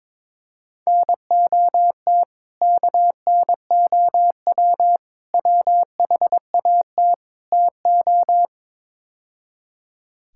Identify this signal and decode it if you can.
Morse code, heard as a keyed tone, about 22 wpm, 700 Hz. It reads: NOT KNOW WHAT TO
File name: practice / 22wpm / Build A Phrase - 5-Word Phrases - 22wpm